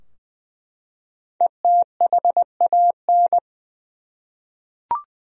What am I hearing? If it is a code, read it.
Morse code, heard as a keyed tone, about 20 wpm, 700 Hz. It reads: ETHAN
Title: USA Common Names - Max 5 Letters - 20wpm